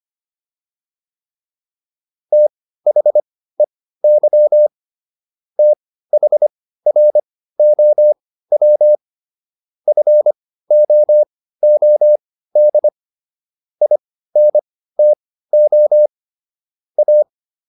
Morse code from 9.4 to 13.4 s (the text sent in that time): FOOD